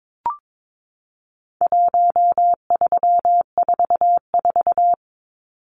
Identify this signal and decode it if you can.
Morse code, heard as a keyed tone, about 22 wpm, 700 Hz. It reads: 1344
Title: Numbers - 4-Digit 22wpm